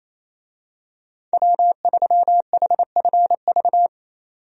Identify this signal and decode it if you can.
Morse code, heard as a keyed tone, about 28 wpm, 700 Hz. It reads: W3HFV
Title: USA Call Signs - 5 Characters - 28wpm